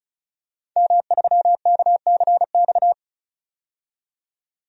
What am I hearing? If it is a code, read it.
Morse code, heard as a keyed tone, about 36 wpm, 700 Hz. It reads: M3KCX
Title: DXCC-100 Call Signs - 5 Characters - 35wpm